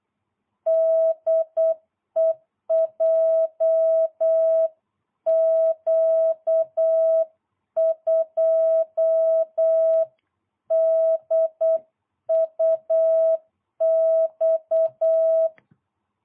Morse code, heard as a keyed tone, about 8 wpm, 660 Hz. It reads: DE JQ2DUX